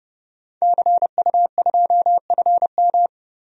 Morse code, heard as a keyed tone, about 32 wpm, 700 Hz. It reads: CU2FM
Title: DXCC-100 Call Signs - 5 Characters - Repeated 3x - 30wpm